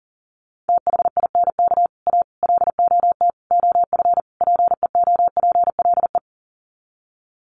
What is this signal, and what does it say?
Morse code, heard as a keyed tone, about 40 wpm, 700 Hz. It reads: THINK A LOT OF PEOPLE
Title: Sets of 5 Words 40wpm